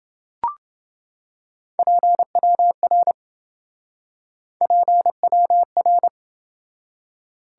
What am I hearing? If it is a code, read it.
Morse code, heard as a keyed tone, about 27 wpm, 700 Hz. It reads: PWR PWR